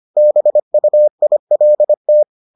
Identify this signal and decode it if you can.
Morse code, heard as a keyed tone, about 25 wpm, 600 Hz. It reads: BUILT